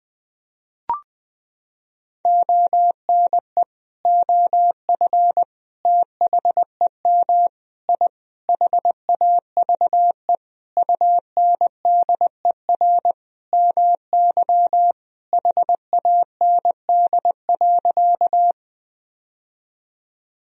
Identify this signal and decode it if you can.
Morse code, heard as a keyed tone, about 20 wpm, 700 Hz. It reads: ONE OF THEM I HAVE UNDER MY HAND.
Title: Sentences from Top 1000 Words - Review Part 1 - 20wpm